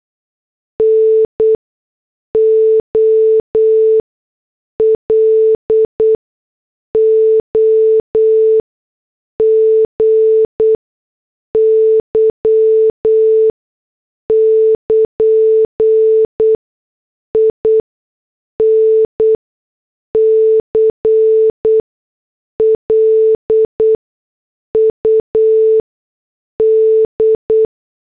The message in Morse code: NOLOGY <KN> INCLUD